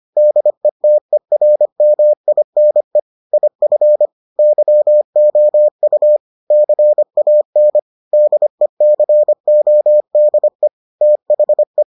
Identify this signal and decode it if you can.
Morse code, heard as a keyed tone, about 25 wpm, 600 Hz. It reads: DETERMINE IF YOU CAN DECODE THE